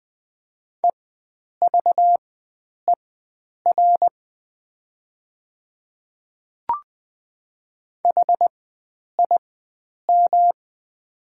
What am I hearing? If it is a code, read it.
Morse code, heard as a keyed tone, about 20 wpm, spaced as for 9 wpm, 700 Hz. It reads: EVER HIM
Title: 4x Character Spacing - Top 500 Words - 20wpm